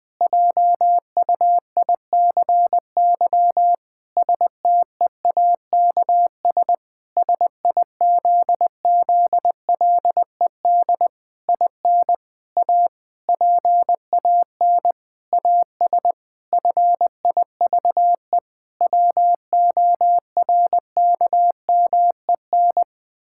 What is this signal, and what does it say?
Morse code, heard as a keyed tone, about 21 wpm, 700 Hz. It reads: JUICY STEAKS SIZZLED IN A PAN AS FIVE WORKMEN
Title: Pangram Challenge v2 - 20wpm